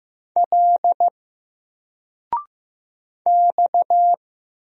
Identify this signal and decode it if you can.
Morse code, heard as a keyed tone, about 15 wpm, 700 Hz. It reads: L X
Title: Single Letter-Number Alt 15wpm